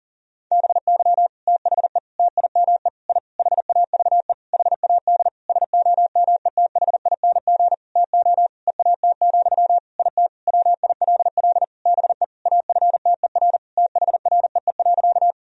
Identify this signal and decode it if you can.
Morse code, heard as a keyed tone, about 40 wpm, 700 Hz. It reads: BY THE TIME I HAVE HAD SOMETHING TO EAT, IT WILL BE AFTER THREE.
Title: Sentences from Top 1000 Words - Review Part 1 - 40wpm